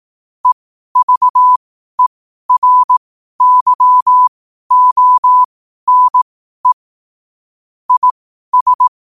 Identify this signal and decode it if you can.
Morse code, heard as a keyed tone, about 18 wpm, spaced as for 12 wpm, 1000 Hz. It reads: EVERYONE IS
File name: qotdfiles / 12wpm